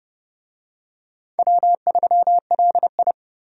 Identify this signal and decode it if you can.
Morse code, heard as a keyed tone, about 30 wpm, 700 Hz. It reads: W3LI
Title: USA Call Signs - 4 Characters - 30wpm